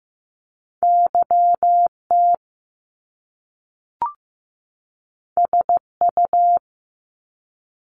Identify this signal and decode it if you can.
Morse code, heard as a keyed tone, about 15 wpm, 700 Hz. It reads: YT SU
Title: Top Two Letter Combinations in Words 15wpm